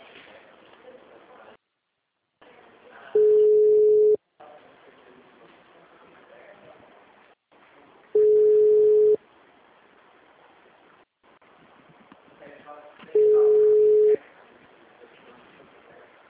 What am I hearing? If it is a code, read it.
Ringback tone (Europe, 425 Hz), with speech.